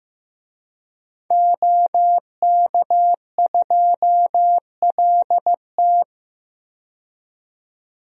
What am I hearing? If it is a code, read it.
Morse code, heard as a keyed tone, about 15 wpm, 700 Hz. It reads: OK2LT
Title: DXCC-100 Call Signs - 5 Characters - 15wpm